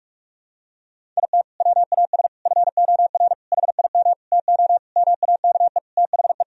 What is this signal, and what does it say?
Morse code, heard as a keyed tone, about 46 wpm, 700 Hz. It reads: IT WAS FOR HIM TO MAKE THE
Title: Sentences from Top 1000 Words - Review Part 1 - 45wpm